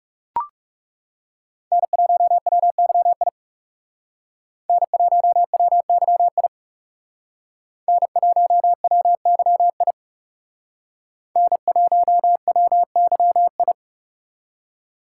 Morse code, heard as a keyed tone, about 36 wpm, 700 Hz. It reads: N1WYI N1WYI N1WYI N1WYI
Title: Speed Racing - USA Call Signs 30wpm